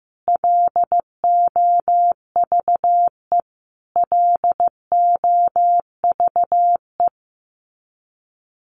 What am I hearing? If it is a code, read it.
Morse code, heard as a keyed tone, about 15 wpm, 700 Hz. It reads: LOVE LOVE
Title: Top 500 Words Repeated 15wpm